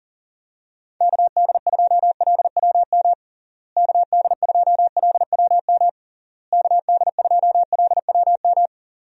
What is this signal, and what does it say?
Morse code, heard as a keyed tone, about 40 wpm, 700 Hz. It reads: KD2LWM KD2LWM KD2LWM
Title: USA Call Signs - 6 Characters - Repeated 3x - 40wpm